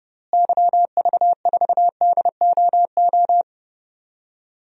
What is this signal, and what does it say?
Morse code, heard as a keyed tone, about 31 wpm, 700 Hz. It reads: YV4DOO
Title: DXCC-100 Call Signs - 6 Characters - Repeated 3x - 30wpm